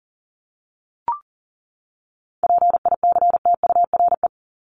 Morse code, heard as a keyed tone, about 40 wpm, 700 Hz. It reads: PICTURE
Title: Top 500 Words 40wpm